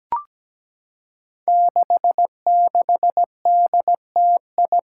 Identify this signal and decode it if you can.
Morse code, heard as a keyed tone, about 17 wpm, 700 Hz. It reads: 66DTI
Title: Five Challenging Letter-Number Combinations 17wpm